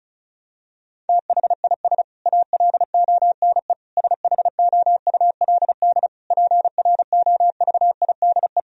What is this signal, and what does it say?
Morse code, heard as a keyed tone, about 36 wpm, 700 Hz. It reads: THIS ALONE SHOULD PROVIDE
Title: Sentences from Top 1000 Words - Review Part 4 - 35wpm